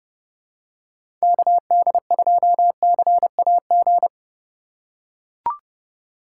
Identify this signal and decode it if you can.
Morse code, heard as a keyed tone, about 30 wpm, 700 Hz. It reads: KD2CAG
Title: USA Call Signs - 4x Character Spacing - 30wpm